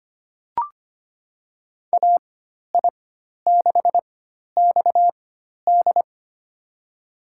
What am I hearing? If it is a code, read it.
Morse code, heard as a keyed tone, about 25 wpm, spaced as for 12 wpm, 700 Hz. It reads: AI6XD
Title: USA Call Signs - 5 Characters - 4x Character Spacing - 25wpm